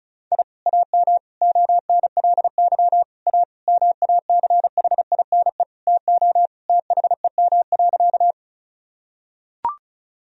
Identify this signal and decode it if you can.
Morse code, heard as a keyed tone, about 36 wpm, 700 Hz. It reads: I AM ONLY A MACHINE TO THEM.
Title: Sentences from Top 1000 Words - Review Part 2 - 35wpm